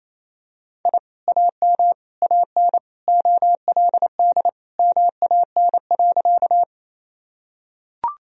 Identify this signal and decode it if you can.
Morse code, heard as a keyed tone, about 28 wpm, 700 Hz. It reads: I AM AN OLD MAN.
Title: Sentences from Top 1000 Words - Review Part 2 - 28wpm